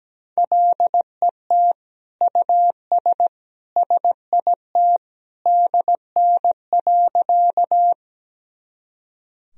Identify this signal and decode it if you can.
Morse code, heard as a keyed tone, about 17 wpm, 700 Hz. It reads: LET US SIT DN.
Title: Sentences from Top 500 Words and Abbreviations 17wpm